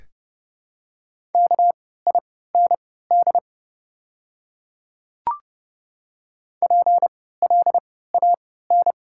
Morse code, heard as a keyed tone, about 29 wpm, spaced as for 17 wpm, 700 Hz. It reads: KIND PLAN
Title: 3x Character Spacing - Top 500 Words - 30wpm